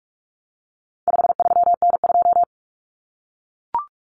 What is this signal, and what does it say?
Morse code, heard as a keyed tone, about 44 wpm, 700 Hz. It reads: 53N2
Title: Common Cut Numbers - 4-Digit 45wpm